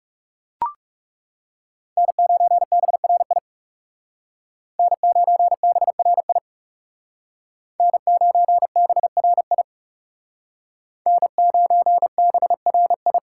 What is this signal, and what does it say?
Morse code, heard as a keyed tone, about 39 wpm, 700 Hz. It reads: N9BRI N9BRI N9BRI N9BRI